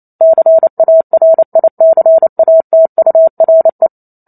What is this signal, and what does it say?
Morse code, heard as a keyed tone, about 28 wpm, 650 Hz. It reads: CARICATURE